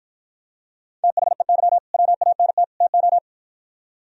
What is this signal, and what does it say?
Morse code, heard as a keyed tone, about 53 wpm, 700 Hz. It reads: THEY WANT TO